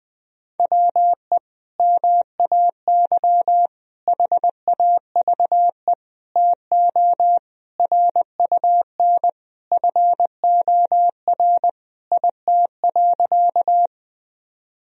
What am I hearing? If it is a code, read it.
Morse code, heard as a keyed tone, about 21 wpm, 700 Hz. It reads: WE MAY HAVE TO RUN FOR IT.
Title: Sentences from Top 1000 Words - Review Part 1 - 20wpm